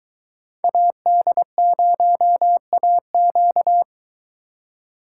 Morse code, heard as a keyed tone, about 23 wpm, 700 Hz. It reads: AD0AQ